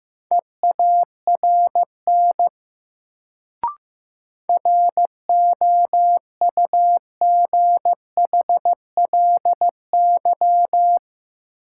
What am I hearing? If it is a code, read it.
Morse code, heard as a keyed tone, about 15 wpm, 700 Hz. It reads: EARN ROUGHLY
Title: Sentences from Top 1500 Words - New Word Drill Alt - 15wpm